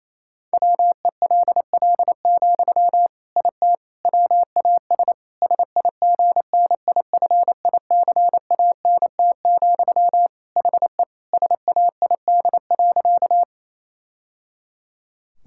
Morse code, heard as a keyed tone, about 28 wpm, 700 Hz. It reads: WELL, IT WAS SIGNIFICANT, HE SAID.
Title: Sentences from Top 1000 Words - Review Part 6 - 28wpm